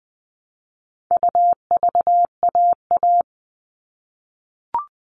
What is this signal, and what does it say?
Morse code, heard as a keyed tone, about 20 wpm, 700 Hz. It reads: UVAA
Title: All Cut Numbers - 4-Digit 20wpm